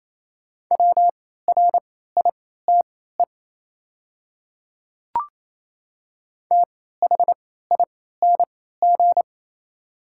Morse code, heard as a keyed tone, about 28 wpm, spaced as for 16 wpm, 690 Hz. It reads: WRITE THING
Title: 3x Character Spacing - Top 100 Words - 28wpm